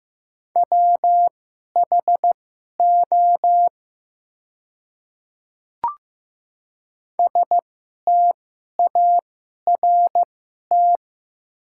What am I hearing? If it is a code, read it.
Morse code, heard as a keyed tone, about 15 wpm, spaced as for 11 wpm, 700 Hz. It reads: WHO START
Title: 2x Character Spacing - Top 500 Words - 15wpm